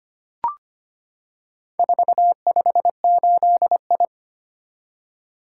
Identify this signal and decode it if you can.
Morse code, heard as a keyed tone, about 25 wpm, 700 Hz. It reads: 458I
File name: practice / 25wpm / Four Letter-Number Combinations - 25wpm